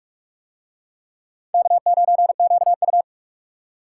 Morse code, heard as a keyed tone, about 46 wpm, 700 Hz. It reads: K9QU